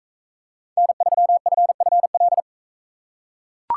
Morse code, heard as a keyed tone, about 42 wpm, 700 Hz. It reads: N3FFL